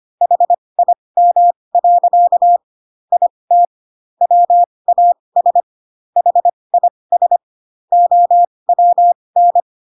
Morse code, heard as a keyed tone, about 25 wpm, spaced as for 20 wpm, 700 Hz. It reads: HIM. IT WAS HIS OWN
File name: Pinocchio / Pinocchio ch7 15 25 ik1hge